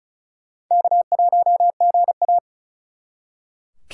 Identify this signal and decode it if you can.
Morse code, heard as a keyed tone, about 46 wpm, 700 Hz. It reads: K1GA